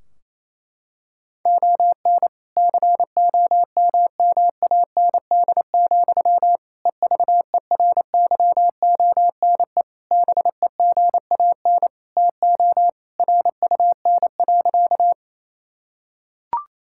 Morse code, heard as a keyed tone, about 28 wpm, 700 Hz. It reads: ON COMMAND, EVERYONE BEGAN TO RUN.
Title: Sentences from Top 1600 Words 28wpm